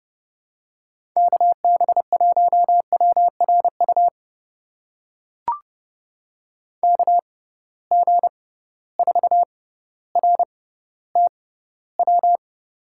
Morse code, heard as a keyed tone, about 31 wpm, spaced as for 15 wpm, 700 Hz. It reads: KB1WRU KG4RTW
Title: USA Call Signs - 6 Characters - 6x Character Spacing - 30wpm